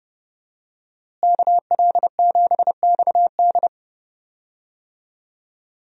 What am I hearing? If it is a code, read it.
Morse code, heard as a keyed tone, about 31 wpm, 700 Hz. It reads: KL7XD
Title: DXCC-100 Call Signs - 5 Characters - Repeated 3x - 30wpm